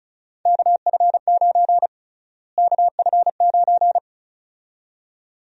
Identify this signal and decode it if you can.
Morse code, heard as a keyed tone, about 36 wpm, 700 Hz. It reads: KF9 KF9
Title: USA Call Sign Prefixes - Repeated 2x - 35wpm